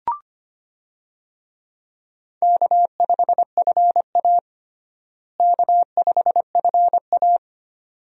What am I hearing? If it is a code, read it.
Morse code, heard as a keyed tone, about 25 wpm, 700 Hz. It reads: K5FA K5FA
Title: USA Call Signs - 4 Characters - Repeated 2x - 25wpm